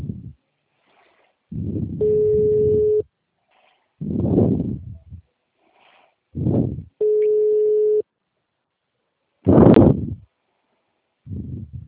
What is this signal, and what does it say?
Ringback tone (Europe, 425 Hz).